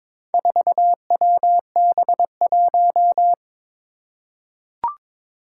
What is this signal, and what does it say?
Morse code, heard as a keyed tone, about 22 wpm, 700 Hz. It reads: 4WB1
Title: Four Challenging Letter-Number Combinations 22wpm